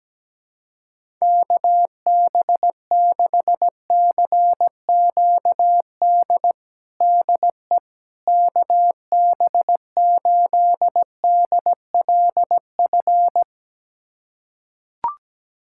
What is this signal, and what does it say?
Morse code, heard as a keyed tone, about 17 wpm, 700 Hz. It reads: KB6CQD DE KB8DLF
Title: USA Callsign DE Callsign 17wpm